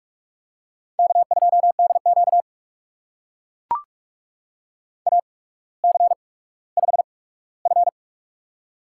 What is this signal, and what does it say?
Morse code, heard as a keyed tone, about 42 wpm, spaced as for 19 wpm, 700 Hz. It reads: K2DQ AC5F
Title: USA Call Signs - 4 Characters - 8x Character Spacing - 45wpm